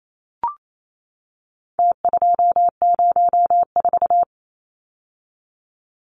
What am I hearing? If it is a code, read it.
Morse code, heard as a keyed tone, about 28 wpm, 700 Hz. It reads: T204
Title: Numbers - Contest Style - 4-Digit 28wpm